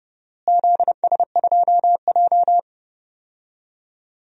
Morse code, heard as a keyed tone, about 30 wpm, 700 Hz. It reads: ZS2J
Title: DXCC-100 Call Signs - 4 Characters - Repeated 3x - 30wpm